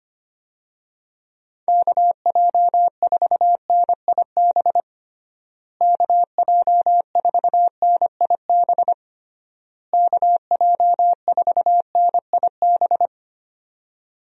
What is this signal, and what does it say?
Morse code, heard as a keyed tone, about 25 wpm, 700 Hz. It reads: KJ4NIB KJ4NIB KJ4NIB
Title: USA Call Signs - 6 Characters - Repeated 3x - 25wpm